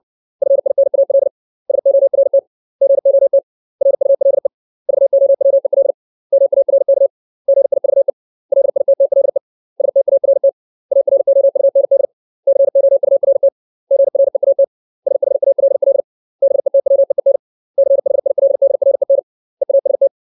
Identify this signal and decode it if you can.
Morse code, heard as a keyed tone, about 60 wpm, 550 Hz. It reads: FINAL SORT GOT RULE VOWEL MARK GIVE LITTLE START AROUND COURT GREAT SHALL BETWEEN CHILDREN EAST
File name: 632 words / 632 Words 60WPM